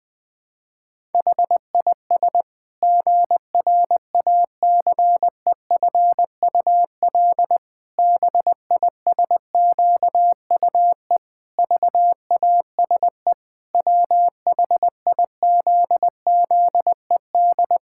Morse code, heard as a keyed tone, about 20 wpm, 700 Hz. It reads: HIS GRACEFUL BISQUE VASE WHIZZED
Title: Pangram Challenge v2 - 20wpm